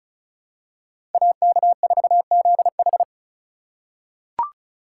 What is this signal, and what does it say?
Morse code, heard as a keyed tone, about 34 wpm, 700 Hz. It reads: AK4ZH
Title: USA Call Signs - 5 Characters - 3x Character Spacing - 35wpm